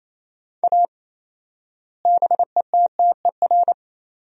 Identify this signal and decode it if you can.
Morse code, heard as a keyed tone, about 28 wpm, 700 Hz. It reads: A BETTER
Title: 4x Word-Spacing - Sentences from Top 400 Words - 28wpm